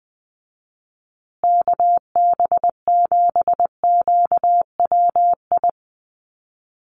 Morse code, heard as a keyed tone, about 20 wpm, 700 Hz. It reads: KB7QWI
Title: USA Call Signs Repeated 3x 20wpm